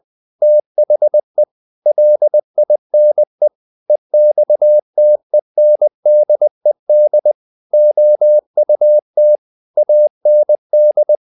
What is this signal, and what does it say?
Morse code, heard as a keyed tone, about 20 wpm, 600 Hz. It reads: THE LINE EXTENDED OUT AND
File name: the old man and the sea / the old man and the sea part 1 20wpm